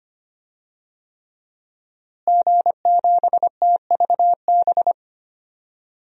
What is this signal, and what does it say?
Morse code, heard as a keyed tone, about 25 wpm, 700 Hz. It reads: G7TVB
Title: DXCC-100 Call Signs - 5 Characters - Repeated 3x - 25wpm